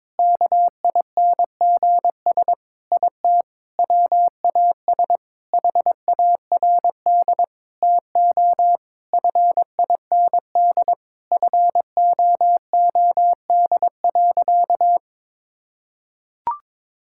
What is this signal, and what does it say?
Morse code, heard as a keyed tone, about 21 wpm, 700 Hz. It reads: KINGS IT WAS HARD TO FIND FOOD.
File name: practice / 22wpm / Sentences with 3-Word Phrases - 22wpm